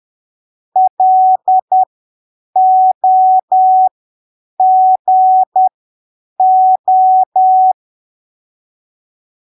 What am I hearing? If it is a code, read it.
Morse code, heard as a keyed tone, about 10 wpm, spaced as for 7 wpm, 750 Hz. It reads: LOGO
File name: arquivos / texto / texto 08 10ppm